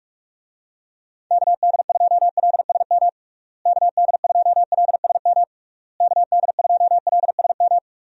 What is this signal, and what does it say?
Morse code, heard as a keyed tone, about 46 wpm, 700 Hz. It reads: KD2LSM KD2LSM KD2LSM
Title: USA Call Signs - 6 Characters - Repeated 3x - 45wpm